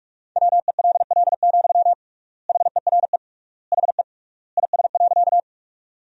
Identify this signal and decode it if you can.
Morse code, heard as a keyed tone, about 42 wpm, 700 Hz. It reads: WELL, HERE HE IS.
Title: 3x Word-Spacing - Sentences from Top 200 Words - 45wpm